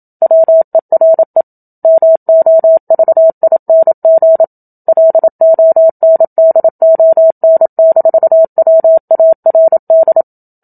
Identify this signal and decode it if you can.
Morse code, heard as a keyed tone, about 27 wpm, 650 Hz. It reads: WERE MOVING LONDON-WARD
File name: Book Two / 01-Chapter 1 - 27 WPM